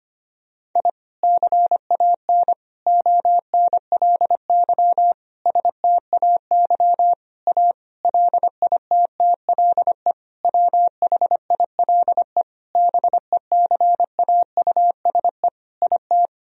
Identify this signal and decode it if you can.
Morse code, heard as a keyed tone, about 25 wpm, 700 Hz. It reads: I CAN ONLY STAY A LITTLE WHILE BECAUSE IT
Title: Sentences from Top 1000 Words - Review Part 3 - 25wpm